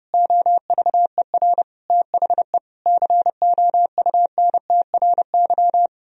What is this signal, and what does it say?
Morse code, heard as a keyed tone, about 28 wpm, 700 Hz. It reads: OVER THE COUNTRY
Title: Speed-Racing - Sets of 4 Words 30wpm